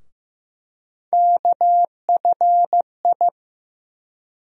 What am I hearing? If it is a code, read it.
Morse code, heard as a keyed tone, about 16 wpm, 700 Hz. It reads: KFI
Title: USA Call Sign Postfixes - 3 Letters 15wpm